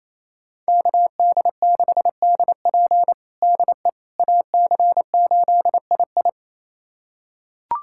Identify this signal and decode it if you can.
Morse code, heard as a keyed tone, about 28 wpm, 700 Hz. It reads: KD6DP DE AC8II
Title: USA Callsign DE Callsign - 5 Characters - 28wpm